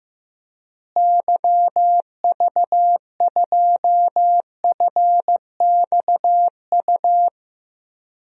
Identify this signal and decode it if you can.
Morse code, heard as a keyed tone, about 15 wpm, 700 Hz. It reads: YV2FXU
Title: DXCC-100 Call Signs - 6 Characters - Repeated 2x - 15wpm